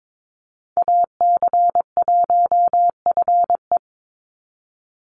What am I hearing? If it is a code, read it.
Morse code, heard as a keyed tone, about 22 wpm, 700 Hz. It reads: AC1FE
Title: Call Signs - 22wpm